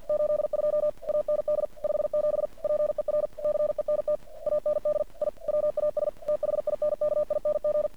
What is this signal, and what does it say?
Morse code, heard as a keyed tone, about 50 wpm, 600 Hz. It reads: 82 AND 57 PER CENT AND I WAS THINKING
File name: archived / CRSnet